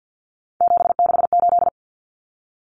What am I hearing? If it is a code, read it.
Morse code, heard as a keyed tone, about 48 wpm, 700 Hz. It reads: 768
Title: Common Cut Numbers - 3-Digit 50wpm